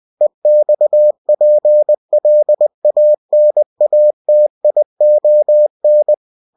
Morse code, heard as a keyed tone, about 20 wpm, 600 Hz. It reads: EXPLANATION
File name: cw-books / winnie-the-pooh / winnie-20wpm-page-0119